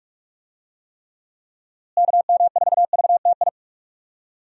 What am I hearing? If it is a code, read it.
Morse code, heard as a keyed tone, about 41 wpm, 700 Hz. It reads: KM4VTI
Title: USA Call Signs - 6 Characters - Repeated 2x - 45wpm